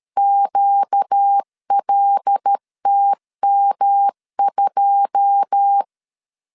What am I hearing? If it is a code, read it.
Morse code, heard as a keyed tone, about 13 wpm, 800 Hz. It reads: QLTM2
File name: LEZIONI / 9 2 12wpm 2301